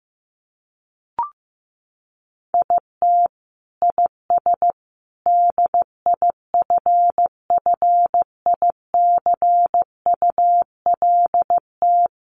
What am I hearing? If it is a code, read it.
Morse code, heard as a keyed tone, about 15 wpm, 700 Hz. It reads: IT IS DIFFICULT
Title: Sentences from Top 1100 Words 15wpm